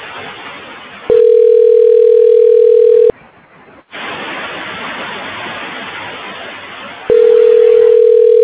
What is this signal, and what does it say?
Ringback tone (North America), 440 + 480 Hz.